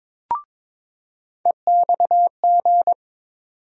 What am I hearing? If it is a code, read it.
Morse code, heard as a keyed tone, about 22 wpm, 700 Hz. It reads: EXG
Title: USA Call Sign Postfixes - 3 Letters 22wpm